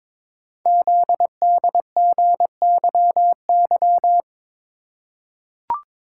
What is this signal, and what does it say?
Morse code, heard as a keyed tone, about 22 wpm, 700 Hz. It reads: ZDGYY
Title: Code Groups - 5 Letters - 22wpm